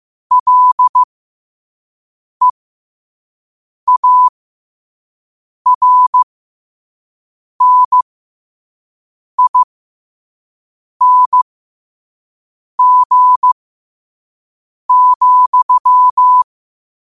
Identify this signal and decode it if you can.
Morse code, heard as a keyed tone, about 15 wpm, spaced as for 5 wpm, 1000 Hz. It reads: LEARNING,
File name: qotdfiles / 5wpm